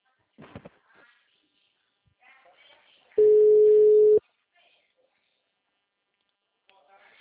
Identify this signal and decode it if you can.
Ringback tone (Europe, 425 Hz).